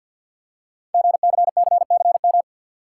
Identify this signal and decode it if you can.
Morse code, heard as a keyed tone, about 47 wpm, 700 Hz. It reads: GCCCM
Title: Code Groups - 5 Letters - 50wpm